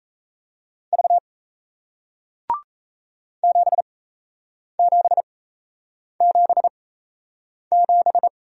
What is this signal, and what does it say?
Morse code, heard as a keyed tone, about 38 wpm, spaced as for 8 wpm, 700 Hz. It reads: V 7777